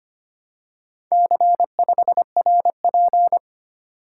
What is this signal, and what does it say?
Morse code, heard as a keyed tone, about 25 wpm, 700 Hz. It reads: C5RP
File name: practice / 25wpm / Four Letter-Number Combinations - 25wpm